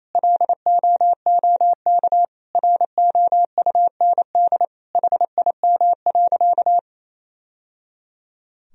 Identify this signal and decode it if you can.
Morse code, heard as a keyed tone, about 28 wpm, 700 Hz. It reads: LOOK ROUND HIM.
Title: Sentences from Top 1000 Words - Review Part 1 - 28wpm